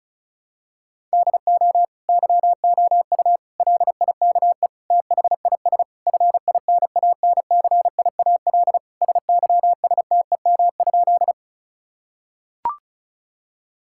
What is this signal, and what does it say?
Morse code, heard as a keyed tone, about 34 wpm, 700 Hz. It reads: DO YOU LIKE THIS FINANCIAL SYSTEM?
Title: Sentences from Top 1000 Words - Review Part 5 - 35wpm